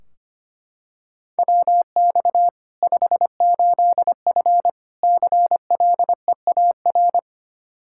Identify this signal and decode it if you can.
Morse code, heard as a keyed tone, about 25 wpm, 700 Hz. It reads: WX 58F CLEAR
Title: Practice QSO - Exchange 2 - Info - 25wpm